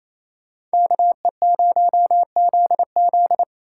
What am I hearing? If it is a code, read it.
Morse code, heard as a keyed tone, about 36 wpm, 700 Hz. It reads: KE0ZZ